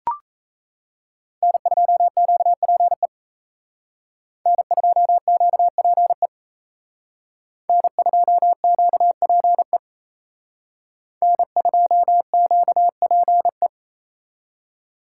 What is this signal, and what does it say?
Morse code, heard as a keyed tone, about 35 wpm, 700 Hz. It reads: N2QPE N2QPE N2QPE N2QPE